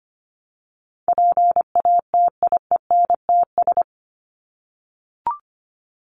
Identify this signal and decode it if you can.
Morse code, heard as a keyed tone, about 25 wpm, 700 Hz. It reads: PATIENTS
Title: ES:S Suffix - Top 100 Words with Base Words 25wpm